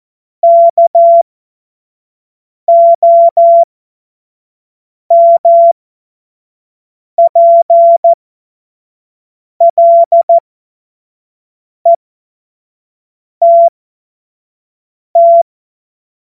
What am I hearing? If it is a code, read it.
Morse code, heard as a keyed tone, about 14 wpm, spaced as for 5 wpm, 680 Hz. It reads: KOMPLETT
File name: Morsekurs / Witze / Witz 28. September 2021, 25 F70